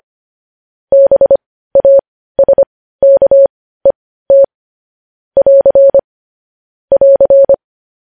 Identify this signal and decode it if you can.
Morse code, heard as a keyed tone, about 25 wpm, spaced as for 15 wpm, 550 Hz. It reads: BASKET <AR> <AR>
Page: news headlines in morse code at 15 wpm